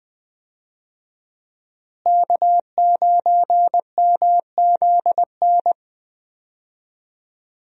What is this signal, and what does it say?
Morse code, heard as a keyed tone, about 20 wpm, 700 Hz. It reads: K9MZN
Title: USA Call Signs - 5 Characters - Repeated 3x - 20wpm